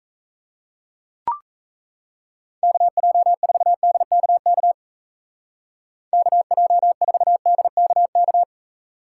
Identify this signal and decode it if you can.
Morse code, heard as a keyed tone, about 34 wpm, 700 Hz. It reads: KJ4DKK KJ4DKK
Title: Speed Racing - USA 2x3 Call Signs 28wpm